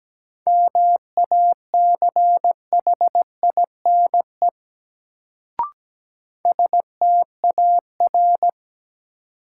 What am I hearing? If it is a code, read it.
Morse code, heard as a keyed tone, about 17 wpm, 700 Hz. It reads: MACHINE STAR
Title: Sentences from Top 1000 Words - Review Part 2 - New Word Drill - 17wpm